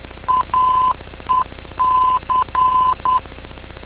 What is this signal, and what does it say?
Morse code, heard as a keyed tone, about 10 wpm, 1020 Hz. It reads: AEC
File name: info / aec vor